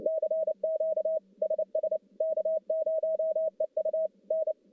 Morse code, heard as a keyed tone, about 29 wpm, 630 Hz. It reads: CQ SS K0EU N
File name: contests / CW-SS-k5tr- -1200